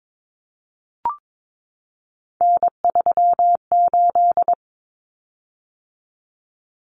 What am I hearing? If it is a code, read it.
Morse code, heard as a keyed tone, about 22 wpm, 700 Hz. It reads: N38
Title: Common Cut Numbers - 3-Digit 22wpm